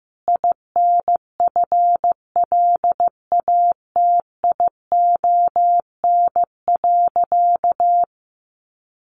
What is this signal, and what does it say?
Morse code, heard as a keyed tone, about 15 wpm, 700 Hz. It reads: INFLATION.
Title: Sentences from Top 1100 Words 15wpm